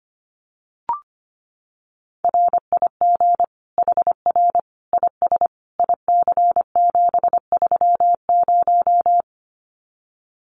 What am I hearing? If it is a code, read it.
Morse code, heard as a keyed tone, about 25 wpm, 700 Hz. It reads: RIG HR IS IC730
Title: Common QSO Phrases 25wpm